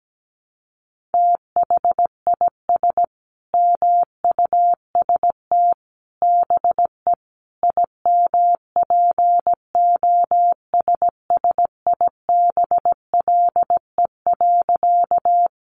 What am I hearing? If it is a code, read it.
Morse code, heard as a keyed tone, about 17 wpm, 700 Hz. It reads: THIS MUST BE IMPOSSIBLE.
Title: Sentences from Top 1100 Words 17wpm